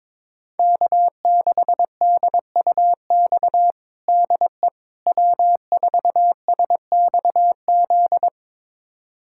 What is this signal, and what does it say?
Morse code, heard as a keyed tone, about 21 wpm, 700 Hz. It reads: K6DUX DE W4SXZ
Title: USA Callsign DE Callsign - 5 Characters - 22wpm